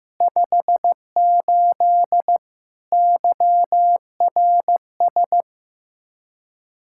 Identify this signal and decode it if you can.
Morse code, heard as a keyed tone, about 15 wpm, 700 Hz. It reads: 58 YRS
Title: Practice QSO - Exchange 3 - More Info - 15wpm